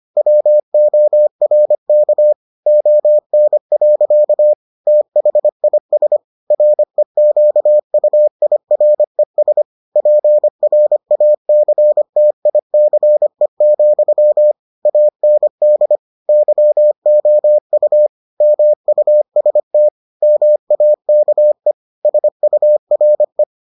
Morse code, heard as a keyed tone, about 25 wpm, 600 Hz. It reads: WORK ON. THIS REQUIRES PRACTICE, AND YOU MUST MAKE SURE